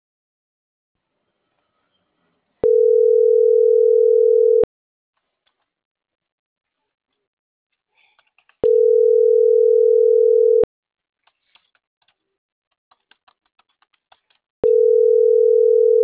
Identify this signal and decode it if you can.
Ringback tone (North America), 440 + 480 Hz.